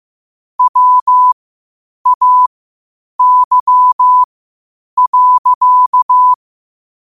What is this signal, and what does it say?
Morse code, heard as a keyed tone, about 15 wpm, spaced as for 8 wpm, 1000 Hz. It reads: WAY.
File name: qotdfiles / 8wpm